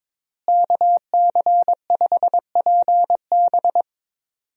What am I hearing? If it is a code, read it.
Morse code, heard as a keyed tone, about 22 wpm, 700 Hz. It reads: KC5PB
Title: USA Call Signs - 5 Characters - Repeated 3x - 22wpm